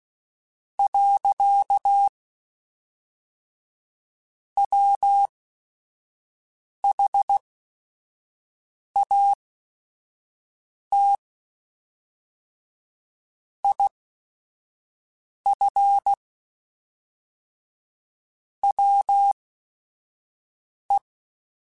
Morse code, heard as a keyed tone, about 16 wpm, spaced as for 5 wpm, 800 Hz. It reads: . WHAT IF WE
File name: Technical Subjects / Technical Subjects-mp3s / AND Gate @5